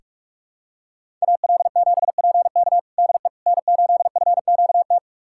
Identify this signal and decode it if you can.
Morse code, heard as a keyed tone, about 47 wpm, 700 Hz. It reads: AL7PK DE N8FQT
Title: USA Callsign DE Callsign - 5 Characters - 45wpm